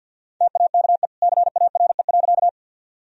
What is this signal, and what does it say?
Morse code, heard as a keyed tone, about 51 wpm, 700 Hz. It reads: TAKE CARE.
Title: Sentences from Top 1000 Words - Review Part 1 - 50wpm